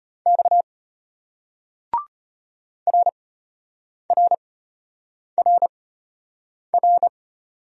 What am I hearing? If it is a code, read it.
Morse code, heard as a keyed tone, about 33 wpm, spaced as for 8 wpm, 700 Hz. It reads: X RRRR